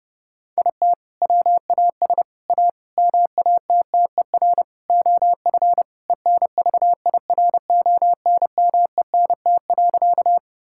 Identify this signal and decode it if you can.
Morse code, heard as a keyed tone, about 30 wpm, 700 Hz. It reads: IT WAS A MATTER OF ENVIRONMENT.
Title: Sentences from Top 1000 Words - Review Part 5 - 30wpm